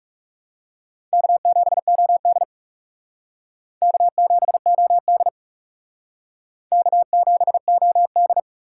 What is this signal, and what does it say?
Morse code, heard as a keyed tone, about 40 wpm, 700 Hz. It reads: K7OD K7OD K7OD